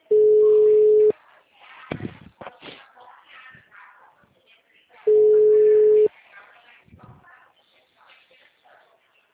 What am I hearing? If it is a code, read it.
Ringback tone (Europe, 425 Hz), with speech.